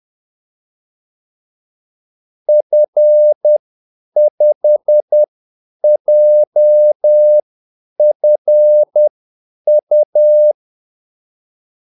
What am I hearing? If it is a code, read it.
Morse code, heard as a keyed tone, about 10 wpm, spaced as for 8 wpm, 600 Hz. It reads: F5JFU
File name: Callsigns-10-8wpm